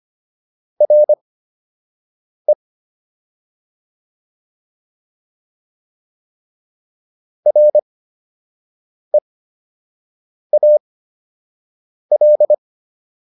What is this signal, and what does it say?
Morse code, heard as a keyed tone, about 25 wpm, spaced as for 6 wpm, 600 Hz. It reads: RE REAL